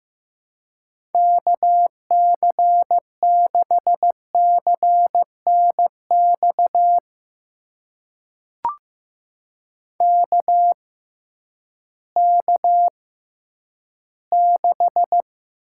Morse code, heard as a keyed tone, about 15 wpm, spaced as for 10 wpm, 700 Hz. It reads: KC6CNX KK6
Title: USA Call Signs - 6 Characters - 6x Character Spacing - 15wpm